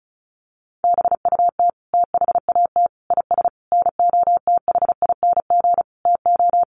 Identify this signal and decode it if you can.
Morse code, heard as a keyed tone, about 34 wpm, 700 Hz. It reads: BUT THAT IS NOTHING TO
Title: Sentences from Top 500 Words 35wpm